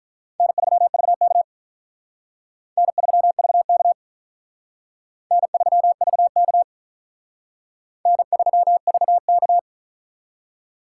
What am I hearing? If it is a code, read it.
Morse code, heard as a keyed tone, about 45 wpm, 700 Hz. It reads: N3VK N3VK N3VK N3VK